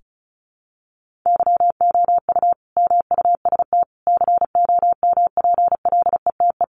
Morse code, heard as a keyed tone, about 34 wpm, 700 Hz. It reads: YOU MUST COMPLETE
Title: Sentences from Top 1400 Words 35wpm